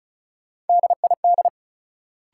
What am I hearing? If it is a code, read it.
Morse code, heard as a keyed tone, about 34 wpm, 700 Hz. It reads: DID
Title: Sentences from Top 100 Words - New Word Drill - 35wpm